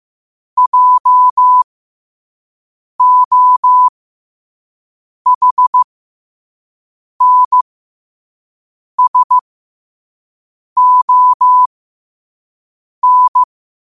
Morse code, heard as a keyed tone, about 15 wpm, spaced as for 5 wpm, 1000 Hz. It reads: JOHNSON
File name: qotdfiles / 5wpm